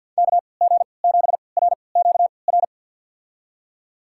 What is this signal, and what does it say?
Morse code, heard as a keyed tone, about 52 wpm, spaced as for 29 wpm, 700 Hz. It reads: KG7RQR
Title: USA Call Signs - 6 Characters - 3x Character Spacing - 50wpm